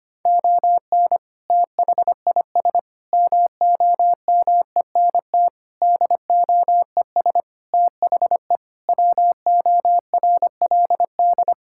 Morse code, heard as a keyed tone, about 25 wpm, 700 Hz. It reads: ON THIS MOMENT DOES THE WORLD
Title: Sentences from Top 1600 Words 25wpm